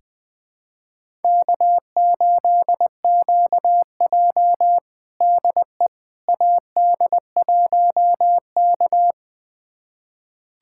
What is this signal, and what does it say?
Morse code, heard as a keyed tone, about 21 wpm, 700 Hz. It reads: K8QJ DE AD1K
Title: USA Callsign DE Callsign - 4 Characters - 20wpm